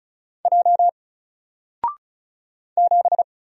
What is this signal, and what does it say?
Morse code, heard as a keyed tone, about 36 wpm, 700 Hz. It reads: J 7
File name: Single-Letter-Number-35wpm